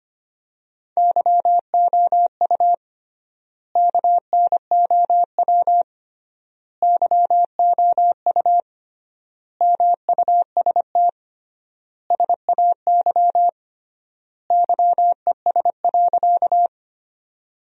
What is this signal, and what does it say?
Morse code, heard as a keyed tone, about 25 wpm, 700 Hz. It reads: YOU KNOW YOU MUST SAY YES.
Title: 3x Word-Spacing - Sentences from Top 500 Words - 25wpm